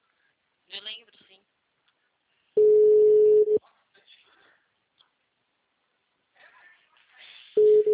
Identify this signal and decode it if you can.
Ringback tone (Europe, 425 Hz), with speech.
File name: out- -2001- - - .48150